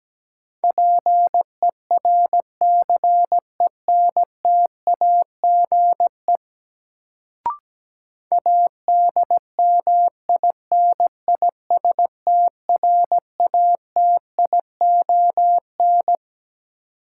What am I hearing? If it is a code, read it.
Morse code, heard as a keyed tone, about 17 wpm, 700 Hz. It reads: PERCENTAGE ADMINISTRATION
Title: Sentences from Top 1300 Words - New Word Drill - 17wpm